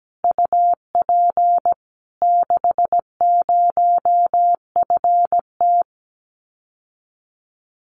Morse code, heard as a keyed tone, about 17 wpm, 700 Hz. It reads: UP 60FT
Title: Common QSO Phrases Encore 17wpm